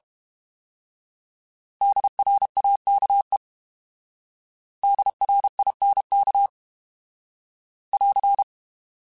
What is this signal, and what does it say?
Morse code, heard as a keyed tone, about 32 wpm, 800 Hz. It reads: DRAKE DRINK <AR>